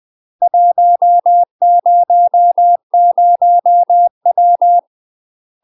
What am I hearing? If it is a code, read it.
Morse code, heard as a keyed tone, about 20 wpm, 700 Hz. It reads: 100W
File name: UK MASTER QSOs TRAINING / 20WPM / UK MASTER QSOs 0013